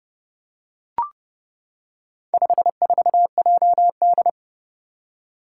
Morse code, heard as a keyed tone, about 30 wpm, 700 Hz. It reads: 54JD
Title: Four Challenging Letter-Number Combinations 30wpm